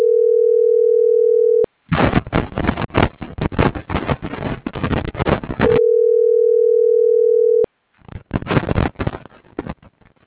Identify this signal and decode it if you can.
Ringback tone (North America), 440 + 480 Hz.